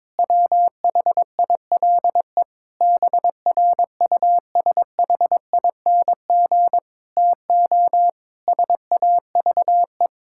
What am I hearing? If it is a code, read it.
Morse code, heard as a keyed tone, about 21 wpm, 700 Hz. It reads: WHILE BRUSHING TO SAVE
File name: Difficult Sentences with Hints - 22wpm